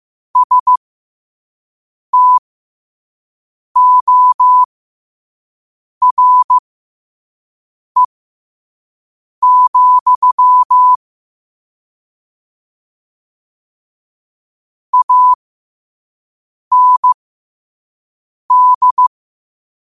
Morse code, heard as a keyed tone, about 15 wpm, spaced as for 5 wpm, 1000 Hz. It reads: STORE, AND
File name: qotdfiles / 5wpm